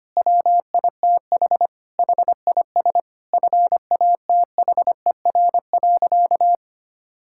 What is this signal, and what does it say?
Morse code, heard as a keyed tone, about 25 wpm, 700 Hz. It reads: WITH HIS FATHER.
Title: Sentences from Top 1000 Words - Review Part 5 - 25wpm